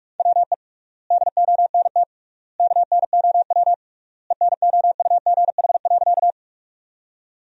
Morse code, heard as a keyed tone, about 41 wpm, 700 Hz. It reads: WE DONT KNOW ENOUGH.
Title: 3x Word-Spacing - Sentences from Top 400 Words - 45wpm